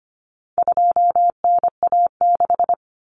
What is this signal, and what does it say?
Morse code, heard as a keyed tone, about 25 wpm, 700 Hz. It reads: 2NA6
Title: Common Cut Numbers - 4-Digit 25wpm